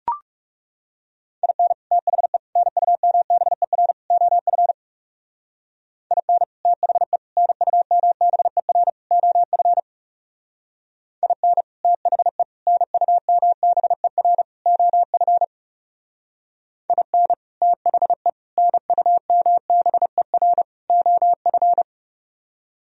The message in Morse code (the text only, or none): IN THE NUMBER OF IN THE NUMBER OF IN THE NUMBER OF IN THE NUMBER OF